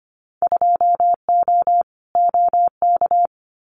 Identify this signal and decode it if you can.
Morse code, heard as a keyed tone, about 25 wpm, 700 Hz. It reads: 2O OK
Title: Winter Field Day Exchanges Encore 25wpm